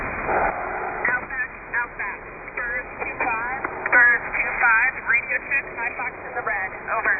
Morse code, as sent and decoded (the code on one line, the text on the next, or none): none